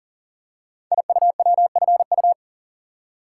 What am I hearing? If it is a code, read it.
Morse code, heard as a keyed tone, about 42 wpm, 690 Hz. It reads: IUWFU
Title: Code Groups - 5 Letters - 40wpm